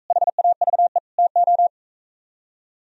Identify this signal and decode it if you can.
Morse code, heard as a keyed tone, about 42 wpm, 700 Hz. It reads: HAVE TO